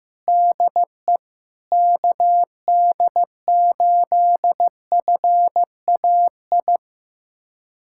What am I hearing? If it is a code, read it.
Morse code, heard as a keyed tone, about 16 wpm, 700 Hz. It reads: DE KD8FAI
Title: USA Callsign DE Callsign 15wpm